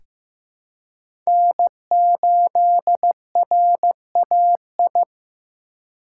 Morse code, heard as a keyed tone, about 15 wpm, 700 Hz. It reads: N8RAI
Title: USA Call Signs - 5 Characters - 4x Character Spacing - 15wpm